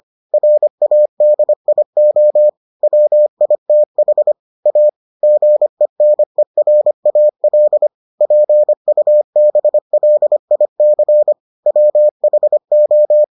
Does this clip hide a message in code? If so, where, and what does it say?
Yes: Morse code from 0.3 to 13.2 s, heard as a keyed tone, about 25 wpm, 600 Hz. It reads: RADIO WITH A GENERAL PUBLIC WHO